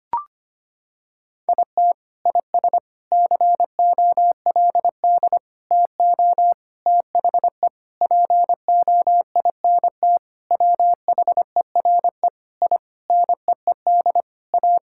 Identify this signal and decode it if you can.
Morse code, heard as a keyed tone, about 25 wpm, 700 Hz. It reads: IT IS COLD TO THE POINT WHERE I NEED A
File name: practice / 25wpm / Sentences with 4-Word Phrases - 25wpm